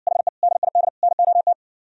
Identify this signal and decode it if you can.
Morse code, heard as a keyed tone, about 59 wpm, 700 Hz. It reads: HE DID NOT